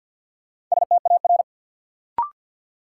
Morse code, heard as a keyed tone, about 53 wpm, 700 Hz. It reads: STAR
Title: Sentences from Top 1000 Words - Review Part 2 - New Word Drill - 50wpm